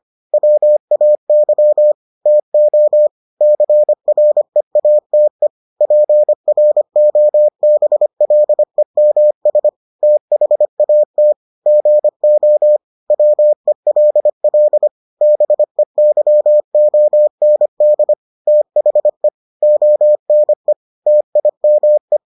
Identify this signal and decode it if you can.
Morse code, heard as a keyed tone, about 25 wpm, 600 Hz. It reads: WAY TO CREATE PROBLEMS THAT GO WELL BEYOND THE ONE TIME